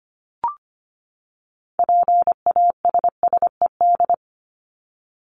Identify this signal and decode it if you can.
Morse code, heard as a keyed tone, about 25 wpm, 700 Hz. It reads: PASSED
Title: ED Suffix - Top 100 Words with Base Words 25wpm